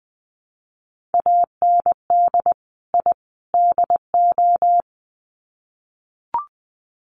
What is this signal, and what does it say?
Morse code, heard as a keyed tone, about 20 wpm, 700 Hz. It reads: AND I DO
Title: Sets of 3 Words 20wpm